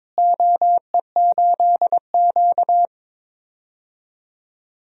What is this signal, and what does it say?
Morse code, heard as a keyed tone, about 23 wpm, 700 Hz. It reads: OE8Q
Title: DXCC-100 Call Signs - 4 Characters - Repeated 3x - 22wpm